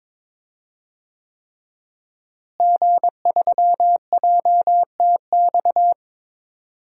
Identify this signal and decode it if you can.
Morse code, heard as a keyed tone, about 22 wpm, 700 Hz. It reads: G3JTX
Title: DXCC-100 Call Signs - 5 Characters - Repeated 3x - 22wpm